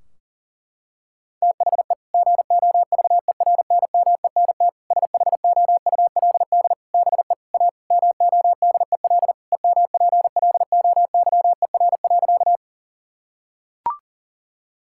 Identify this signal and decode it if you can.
Morse code, heard as a keyed tone, about 40 wpm, 700 Hz. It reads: THE GOVERNMENT SHOULD BE A MODEL EMPLOYER.
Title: Sentences from Top 1600 Words 40wpm